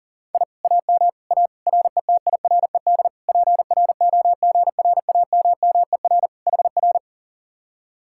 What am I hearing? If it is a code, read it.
Morse code, heard as a keyed tone, about 50 wpm, 700 Hz. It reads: I AM A RETIRED PROGRAMMER HR
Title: Speed-Racing Common QSO Phrases - 40wpm